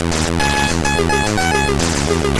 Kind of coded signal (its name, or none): none